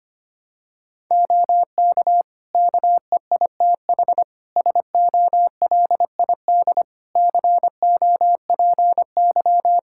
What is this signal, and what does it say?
Morse code, heard as a keyed tone, about 25 wpm, 700 Hz. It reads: OK KEITH SOLID COPY
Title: Practice QSO - Exchange 3 - More Info - 25wpm